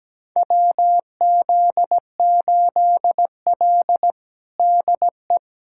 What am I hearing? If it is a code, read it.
Morse code, heard as a keyed tone, about 17 wpm, 700 Hz. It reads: WZ8L DE
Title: USA Callsign DE Callsign - 4 Characters - 17wpm